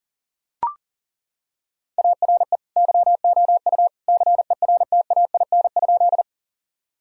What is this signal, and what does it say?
Morse code, heard as a keyed tone, about 37 wpm, 700 Hz. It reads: ARE YOU CERTAIN?
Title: Sentences from Top 1000 Words - Review Part 2 - 40wpm